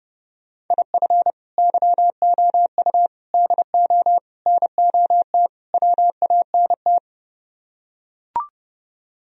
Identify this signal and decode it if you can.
Morse code, heard as a keyed tone, about 30 wpm, 700 Hz. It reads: IF YOU DO NOT WANT